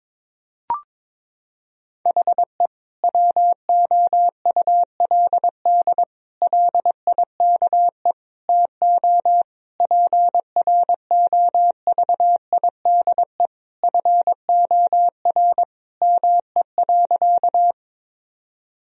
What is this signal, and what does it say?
Morse code, heard as a keyed tone, about 22 wpm, 700 Hz. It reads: HE WOULD LIKE TO PROVIDE FOR ME.
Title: Sentences from Top 1000 Words - Review Part 3 - 22wpm